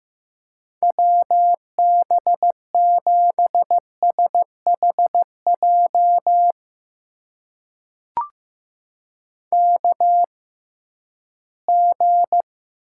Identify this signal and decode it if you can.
Morse code, heard as a keyed tone, about 15 wpm, spaced as for 12 wpm, 700 Hz. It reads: WB7SHJ KG
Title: USA Call Signs - 6 Characters - 6x Character Spacing - 15wpm